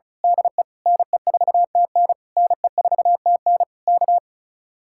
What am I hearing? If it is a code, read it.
Morse code, heard as a keyed tone, about 33 wpm, 700 Hz. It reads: DE NE4TN NE4TN K
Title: Speed-Racing Common QSO Phrases Encore - 35wpm